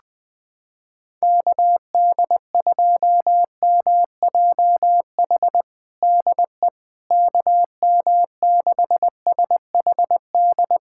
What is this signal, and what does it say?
Morse code, heard as a keyed tone, about 20 wpm, 700 Hz. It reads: KD2MJH DE KM6SHD
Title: USA Callsign DE Callsign 20wpm